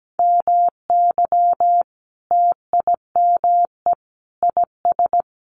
Morse code, heard as a keyed tone, about 17 wpm, 700 Hz. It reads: MY TIME IS
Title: Sentences from Top 1300 Words 17wpm